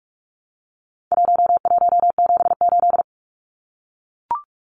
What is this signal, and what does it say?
Morse code, heard as a keyed tone, about 44 wpm, 700 Hz. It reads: 1178
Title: Numbers - 4-Digit 45wpm